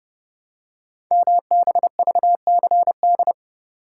Morse code, heard as a keyed tone, about 30 wpm, 700 Hz. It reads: MBVCD
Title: Code Groups - 5 Letters - 30wpm